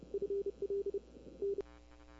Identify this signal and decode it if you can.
Morse code, heard as a keyed tone, about 30 wpm, 400 Hz. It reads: FL N
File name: CW / 2005-k5tr arrl cw dx -20:00